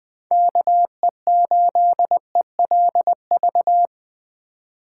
Morse code, heard as a keyed tone, about 24 wpm, 700 Hz. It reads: KE8ELV
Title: Speed Racing - USA 2x3 Call Signs 20wpm